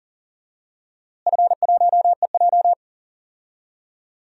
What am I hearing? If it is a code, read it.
Morse code, heard as a keyed tone, about 40 wpm, 700 Hz. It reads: F1EJ
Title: DXCC-100 Call Signs - 4 Characters - Repeated 3x - 40wpm